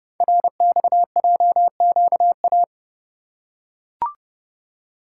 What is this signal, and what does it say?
Morse code, heard as a keyed tone, about 30 wpm, 700 Hz. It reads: RXJQA
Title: Code Groups - 5 Letters - 30wpm